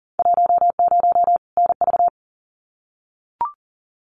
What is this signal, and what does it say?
Morse code, heard as a keyed tone, about 38 wpm, 700 Hz. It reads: 10 NV
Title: Winter Field Day Exchanges 40wpm